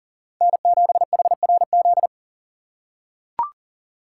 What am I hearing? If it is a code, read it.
Morse code, heard as a keyed tone, about 40 wpm, 700 Hz. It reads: N7HRZ
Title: USA Call Signs - 5 Characters - 2x Character Spacing - 40wpm